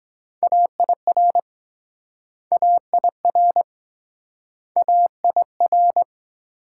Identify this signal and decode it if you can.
Morse code, heard as a keyed tone, about 23 wpm, 700 Hz. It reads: AIR AIR AIR